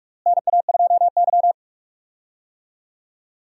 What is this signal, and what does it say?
Morse code, heard as a keyed tone, about 42 wpm, 700 Hz. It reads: NA2Y
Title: USA Call Signs - 4 Characters - Repeated 3x - 45wpm